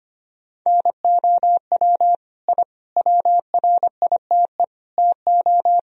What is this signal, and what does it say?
Morse code, heard as a keyed tone, about 25 wpm, 700 Hz. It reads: NOW I WRITE TO
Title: Sentences from Top 1000 Words - Review Part 1 - 25wpm